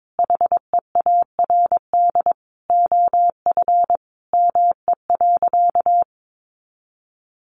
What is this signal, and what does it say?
Morse code, heard as a keyed tone, about 22 wpm, 700 Hz. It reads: HEARD OF ME.
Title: Sentences from Top 1300 Words 22wpm